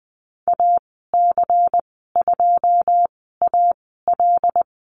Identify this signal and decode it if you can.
Morse code, heard as a keyed tone, about 20 wpm, spaced as for 15 wpm, 700 Hz. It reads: AC2AL
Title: 122 Lesson 20g - Number 3 - Call Signs 20wpm